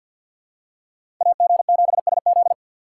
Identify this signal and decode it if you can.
Morse code, heard as a keyed tone, about 52 wpm, 700 Hz. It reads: AG7SZ
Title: USA Call Signs - 5 Characters - 3x Character Spacing - 50wpm